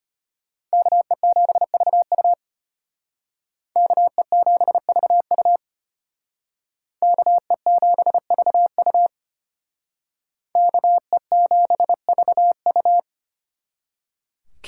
Morse code, heard as a keyed tone, about 32 wpm, 700 Hz. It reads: KE7VU KE7VU KE7VU KE7VU